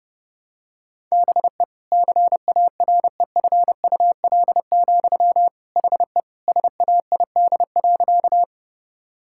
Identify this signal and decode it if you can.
Morse code, heard as a keyed tone, about 32 wpm, 700 Hz. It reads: BE CAREFUL, HE SAID.
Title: Sentences from Top 1600 Words 30wpm